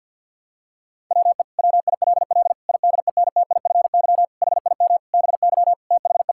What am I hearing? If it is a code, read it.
Morse code, heard as a keyed tone, about 50 wpm, 700 Hz. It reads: WE WILL IDENTIFY HIM BY THE
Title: Sentences from Top 1000 Words - Review Part 6 - 50wpm